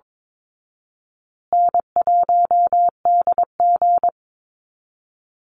Morse code, heard as a keyed tone, about 22 wpm, 700 Hz. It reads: N1DG
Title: Easy Call Signs Repeated 2x - 22wpm